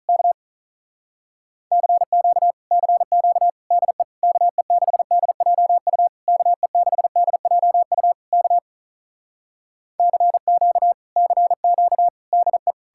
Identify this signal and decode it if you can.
Morse code, heard as a keyed tone, about 46 wpm, 700 Hz. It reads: K CQ CQ DE KE6DJU KE6DJU K CQ CQ DE